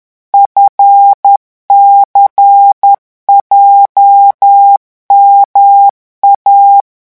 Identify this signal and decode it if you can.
Morse code, heard as a keyed tone, about 10 wpm, 800 Hz. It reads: FCJMA